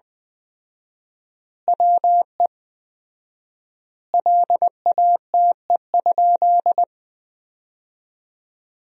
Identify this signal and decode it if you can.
Morse code, heard as a keyed tone, about 20 wpm, 700 Hz. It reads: WE LATE?
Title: 4x Word-Spacing - Sentences from Top 300 Words - 20wpm